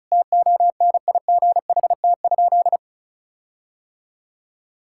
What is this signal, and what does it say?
Morse code, heard as a keyed tone, about 34 wpm, 700 Hz. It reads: TONIGHT?
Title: Sentences from Top 1600 Words 35wpm